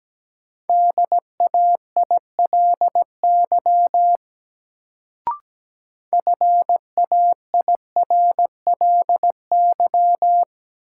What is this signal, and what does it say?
Morse code, heard as a keyed tone, about 17 wpm, 700 Hz. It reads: DAILY FAIRLY
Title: Sentences from Top 1100 Words - New Word Drill - 17wpm